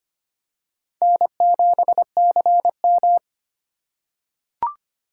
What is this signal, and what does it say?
Morse code, heard as a keyed tone, about 25 wpm, 700 Hz. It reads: N7CM
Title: USA Call Signs - 4 Characters - 3x Character Spacing - 25wpm